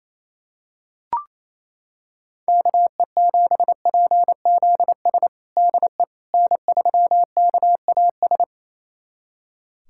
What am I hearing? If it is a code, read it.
Morse code, heard as a keyed tone, about 28 wpm, 700 Hz. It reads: KE7PZS DE N3KAS
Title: USA Callsign DE Callsign 28wpm